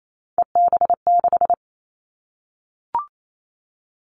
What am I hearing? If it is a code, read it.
Morse code, heard as a keyed tone, about 28 wpm, 700 Hz. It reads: EB6
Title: Callsign Prefixes for Top 50 DXCC - 28wpm